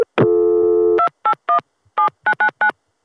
DTMF, heard as telephone touch tones, tones of about 90 ms. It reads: 2817999